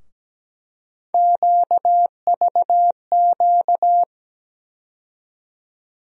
Morse code, heard as a keyed tone, about 17 wpm, 700 Hz. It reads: QVQ
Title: USA Call Sign Postfixes - 3 Letters 17wpm